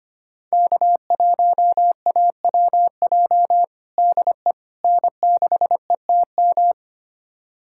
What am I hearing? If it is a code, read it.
Morse code, heard as a keyed tone, about 25 wpm, 700 Hz. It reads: K1AWJ DE N6ETM
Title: USA Callsign DE Callsign 25wpm